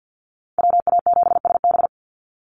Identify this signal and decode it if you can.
Morse code, heard as a keyed tone, about 50 wpm, 700 Hz. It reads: WA7SB
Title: NA SOTA Chaser Callsigns 50wpm